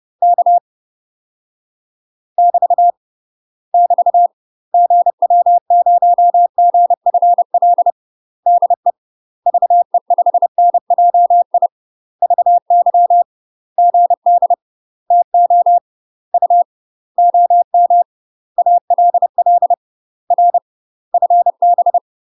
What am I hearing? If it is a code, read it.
Morse code, heard as a keyed tone, about 30 wpm, 700 Hz. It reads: K <BT> <BT> GW0GFL DE VE5NJI VY GD TO U OM ALL R FB
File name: UK MASTER QSOs TRAINING / 30WPM / UK MASTER QSOs 0035